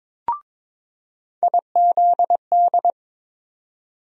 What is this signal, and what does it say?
Morse code, heard as a keyed tone, about 22 wpm, 700 Hz. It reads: IZD
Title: USA Call Sign Postfixes - 3 Letters 22wpm